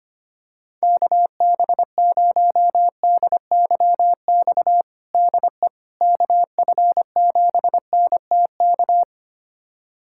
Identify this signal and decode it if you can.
Morse code, heard as a keyed tone, about 25 wpm, 700 Hz. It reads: KB0DYX DE KF7NTK
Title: USA Callsign DE Callsign - 6 Characters - 25wpm